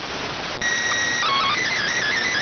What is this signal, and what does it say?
SSTV (slow-scan television), PD 290.